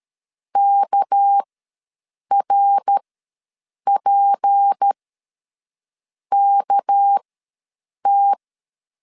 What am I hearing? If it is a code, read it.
Morse code, heard as a keyed tone, about 13 wpm, spaced as for 6 wpm, 800 Hz. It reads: KRPKT
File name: LEZIONI / 6 PKX0 / 6 PK ws 12wpm 3116